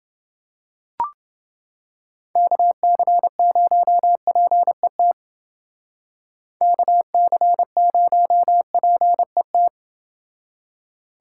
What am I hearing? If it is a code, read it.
Morse code, heard as a keyed tone, about 28 wpm, 700 Hz. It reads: KC0PET KC0PET